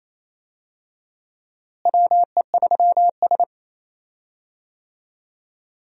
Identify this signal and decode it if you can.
Morse code, heard as a keyed tone, about 28 wpm, 700 Hz. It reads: WE3S
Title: USA Call Signs - 4 Characters - Repeated 2x - 28wpm